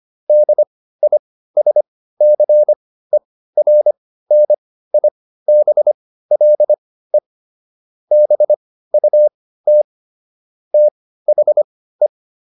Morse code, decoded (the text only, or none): DISCERNIBLE BUT THE